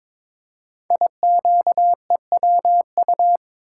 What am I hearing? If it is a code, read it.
Morse code, heard as a keyed tone, about 22 wpm, 700 Hz. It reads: IQEWU
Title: Code Groups - 5 Letters - 22wpm